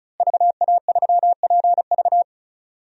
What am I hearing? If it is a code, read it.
Morse code, heard as a keyed tone, about 34 wpm, 700 Hz. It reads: VA3PV
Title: DXCC-100 Call Signs - 5 Characters - Repeated 3x - 35wpm